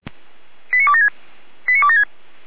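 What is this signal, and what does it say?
Five-tone selective call (ZVEI), heard as quick runs of five tones, tones of about 70 ms. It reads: 89276 89276
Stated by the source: ZVEI five-tone selective call received on an SDR.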